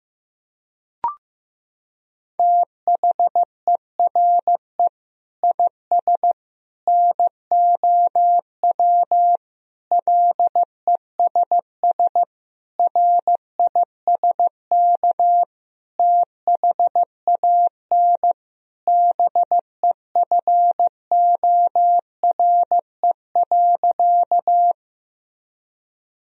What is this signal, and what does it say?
Morse code, heard as a keyed tone, about 15 wpm, 700 Hz. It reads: THERE IS NOW LESS RISK THAN BEFORE.
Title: Sentences from Top 1000 Words - Review Part 4 - 15wpm